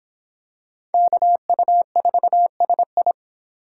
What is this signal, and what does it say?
Morse code, heard as a keyed tone, about 25 wpm, 700 Hz. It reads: KU4SI